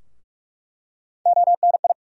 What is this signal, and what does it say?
Morse code, heard as a keyed tone, about 46 wpm, 700 Hz. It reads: ONI
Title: USA Call Sign Postfixes - 3 Letters 45wpm